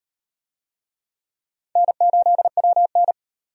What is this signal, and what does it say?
Morse code, heard as a keyed tone, about 38 wpm, 700 Hz. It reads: N8WN